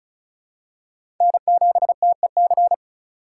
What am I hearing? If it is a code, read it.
Morse code, heard as a keyed tone, about 36 wpm, 700 Hz. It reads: N7TEC
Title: USA Call Signs - 5 Characters - 2x Character Spacing - 35wpm